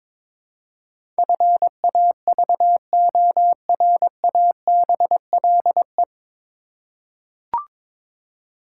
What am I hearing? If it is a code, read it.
Morse code, heard as a keyed tone, about 23 wpm, 700 Hz. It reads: FAVORABLE
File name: 9 Letter Words - First 3 Letters Given - 22wpm